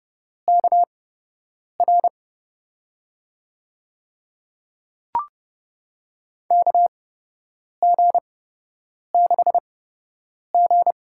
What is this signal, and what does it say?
Morse code, heard as a keyed tone, about 30 wpm, spaced as for 8 wpm, 700 Hz. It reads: KR KG6G
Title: USA Call Signs - 6 Characters - 8x Character Spacing - 30wpm